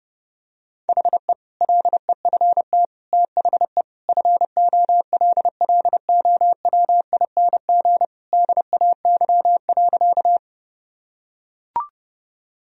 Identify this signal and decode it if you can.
Morse code, heard as a keyed tone, about 32 wpm, 700 Hz. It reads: HE LEFT THE FOLLOWING DAY.
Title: Sentences from Top 1000 Words - Review Part 4 - 30wpm